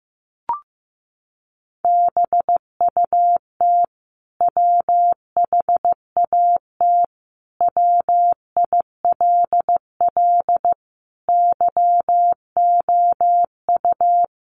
Morse code, heard as a keyed tone, about 15 wpm, 700 Hz. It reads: BUT WHAT WILL YOU
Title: Sentences from Top 100 Words 15wpm